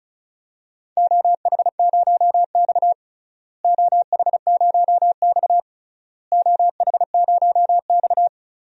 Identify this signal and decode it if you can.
Morse code, heard as a keyed tone, about 34 wpm, 700 Hz. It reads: OH0X OH0X OH0X
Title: DXCC-100 Call Signs - 4 Characters - Repeated 3x - 35wpm